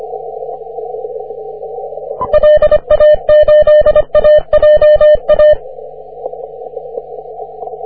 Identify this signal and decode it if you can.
Morse code, heard as a keyed tone, about 25 wpm, 600 Hz. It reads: LA8AJA
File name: wp / e51nou 30m cw